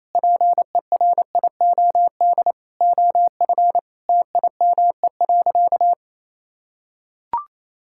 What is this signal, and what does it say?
Morse code, heard as a keyed tone, about 27 wpm, 700 Hz. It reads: PERIOD OF TIME.
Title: Sentences from Top 1000 Words - Review Part 5 - 28wpm